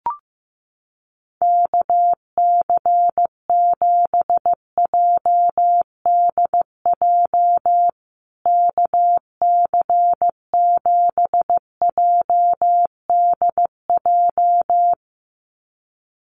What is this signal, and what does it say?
Morse code, heard as a keyed tone, about 15 wpm, 700 Hz. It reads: KC7JDJ KC7JDJ
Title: USA Call Signs Repeated 2x 15wpm